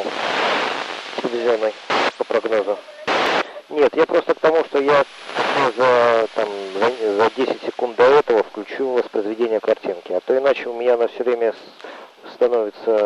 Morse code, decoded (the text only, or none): none